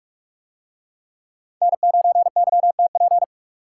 Morse code, heard as a keyed tone, about 41 wpm, 700 Hz. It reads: N9YTP